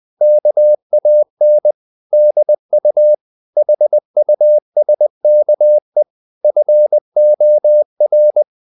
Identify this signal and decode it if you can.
Morse code, heard as a keyed tone, about 20 wpm, 600 Hz. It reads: KAN DU HUSKE FOR
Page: Ferie 20 wpm